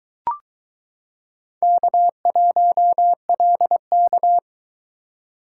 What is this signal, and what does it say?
Morse code, heard as a keyed tone, about 22 wpm, 700 Hz. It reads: K1LK